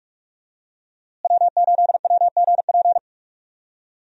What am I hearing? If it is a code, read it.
Morse code, heard as a keyed tone, about 46 wpm, 700 Hz. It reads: W8WGP
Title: USA Call Signs - 5 Characters - Repeated 3x - 45wpm